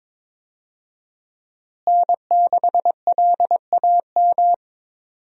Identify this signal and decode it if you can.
Morse code, heard as a keyed tone, about 21 wpm, 700 Hz. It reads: N6LAM
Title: USA Call Signs - 5 Characters - Repeated 3x - 22wpm